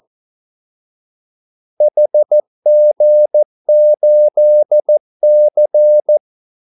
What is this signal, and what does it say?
Morse code, heard as a keyed tone, about 14 wpm, 600 Hz. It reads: HG8C
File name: Callsigns-14wpm